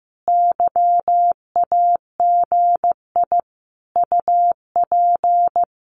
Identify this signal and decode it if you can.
Morse code, heard as a keyed tone, about 15 wpm, 700 Hz. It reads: YAGI UP
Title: Common QSO Phrases Encore 15wpm